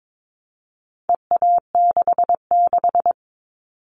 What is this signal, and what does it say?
Morse code, heard as a keyed tone, about 22 wpm, 700 Hz. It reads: EA66
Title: All Cut Numbers - 4-Digit 22wpm